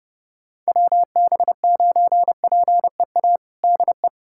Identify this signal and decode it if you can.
Morse code, heard as a keyed tone, about 30 wpm, 700 Hz. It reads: WB9PEA DE
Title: USA Callsign DE Callsign - 6 Characters - 30wpm